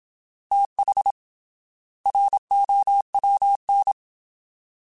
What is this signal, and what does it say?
Morse code, heard as a keyed tone, about 27 wpm, 800 Hz. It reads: TH ROWN
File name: TH / TH mp3s / Fast TH @25